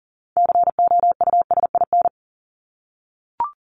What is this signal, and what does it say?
Morse code, heard as a keyed tone, about 40 wpm, 700 Hz. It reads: COUSIN
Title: Top 1501-1750 Words 40wpm